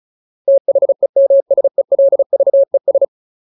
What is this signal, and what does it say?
Morse code, heard as a keyed tone, about 34 wpm, 550 Hz. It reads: THEMSELVES
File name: Ultimate Word List / 35WPM Ultimate